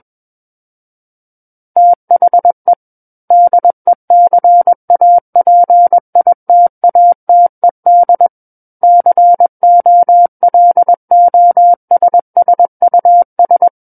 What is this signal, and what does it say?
Morse code, heard as a keyed tone, about 21 wpm, 700 Hz. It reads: THE DECAPITATED COLOSSUS
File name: Book One / 12-Chapter 12 - 21 WPM